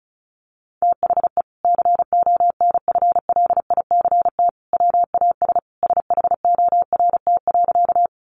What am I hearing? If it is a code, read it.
Morse code, heard as a keyed tone, about 34 wpm, 700 Hz. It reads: THE CONFLICT WAS SHORT.
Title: Sentences from Top 1300 Words 35wpm